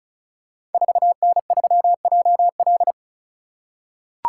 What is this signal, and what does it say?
Morse code, heard as a keyed tone, about 36 wpm, 700 Hz. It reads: 4N3JL
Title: Code Groups - 5 Letter:Numbers - 35wpm